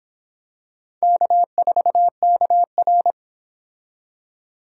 Morse code, heard as a keyed tone, about 25 wpm, 700 Hz. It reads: K4KR